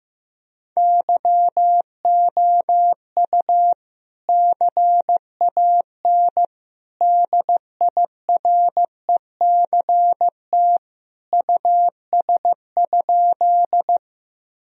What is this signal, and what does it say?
Morse code, heard as a keyed tone, about 15 wpm, 700 Hz. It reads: YOU CAN DIRECT US?
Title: Sentences from Top 1000 Words - Review Part 2 - 15wpm